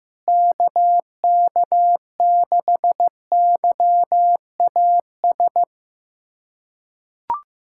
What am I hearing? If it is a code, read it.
Morse code, heard as a keyed tone, about 15 wpm, 700 Hz. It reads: KK6YAS
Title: USA Callsign DE Callsign - 6 Characters - 15wpm